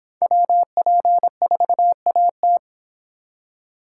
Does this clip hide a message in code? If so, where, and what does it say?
Yes: Morse code from 0.2 to 2.6 s, heard as a keyed tone, about 26 wpm, 700 Hz. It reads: WP4AT